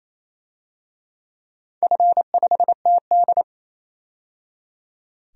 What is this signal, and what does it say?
Morse code, heard as a keyed tone, about 28 wpm, 700 Hz. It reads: F5TD
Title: DXCC-100 Call Signs - 4 Characters - 28wpm